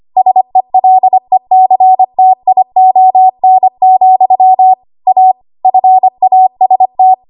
Morse code, heard as a keyed tone, about 25 wpm, 750 Hz. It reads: SELECTION, A FAST
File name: Archive / 25 WPM / 25WPM